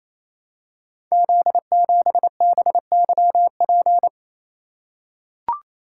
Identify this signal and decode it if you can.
Morse code, heard as a keyed tone, about 27 wpm, 700 Hz. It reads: Z7BYP
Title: Code Groups - 5 Letter:Numbers - 28wpm